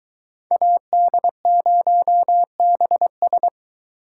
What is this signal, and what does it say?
Morse code, heard as a keyed tone, about 22 wpm, 700 Hz. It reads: AD0BS